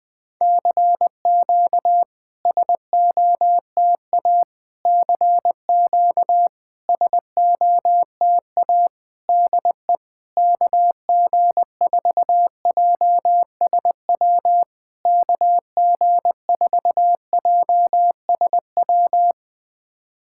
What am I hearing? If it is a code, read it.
Morse code, heard as a keyed tone, about 21 wpm, 700 Hz. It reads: CQ SOTA CQ SOTA DE KG4JSW KG4JSW
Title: SOTA QSO Practice 20wpm